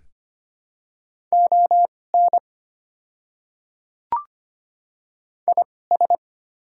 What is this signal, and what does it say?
Morse code, heard as a keyed tone, about 25 wpm, spaced as for 18 wpm, 690 Hz. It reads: ON IS
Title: 2x Character Spacing - Top 100 Words - 25wpm